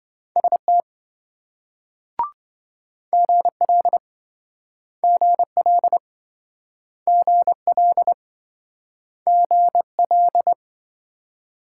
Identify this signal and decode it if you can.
Morse code, heard as a keyed tone, about 28 wpm, 700 Hz. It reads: ST GL GL GL GL